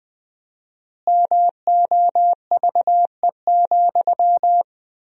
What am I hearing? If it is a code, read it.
Morse code, heard as a keyed tone, about 20 wpm, 700 Hz. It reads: MOVE,
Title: 4x Word-Spacing - Sentences from Top 300 Words - 20wpm